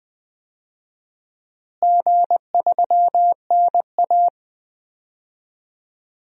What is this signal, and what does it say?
Morse code, heard as a keyed tone, about 20 wpm, 700 Hz. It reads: G3NA
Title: DXCC-100 Call Signs - 4 Characters - Repeated 2x - 20wpm